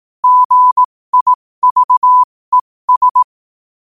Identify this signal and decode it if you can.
Morse code, heard as a keyed tone, about 18 wpm, spaced as for 15 wpm, 1000 Hz. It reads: GIVES
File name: qotdfiles / 15wpm